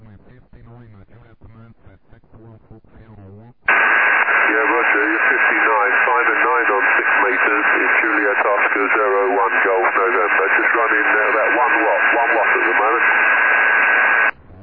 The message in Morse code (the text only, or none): none